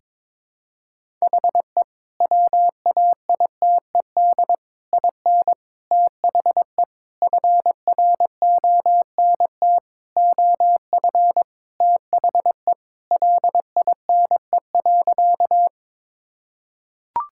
Morse code, heard as a keyed tone, about 22 wpm, 700 Hz. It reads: HE WAITED IN THE FRONT OF THE LINE.
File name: practice / 22wpm / Sentences with 4-Word Phrases - 22wpm